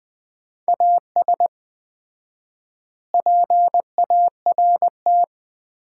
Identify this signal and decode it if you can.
Morse code, heard as a keyed tone, about 21 wpm, 700 Hz. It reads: AS PART
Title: 4x Word-Spacing - Sets of 4 Words - 20wpm